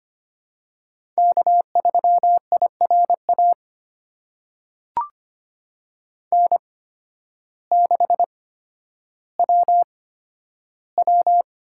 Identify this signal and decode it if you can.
Morse code, heard as a keyed tone, about 25 wpm, spaced as for 12 wpm, 700 Hz. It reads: K3IRA N6WW
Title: USA Call Signs - 5 Characters - 8x Character Spacing - 25wpm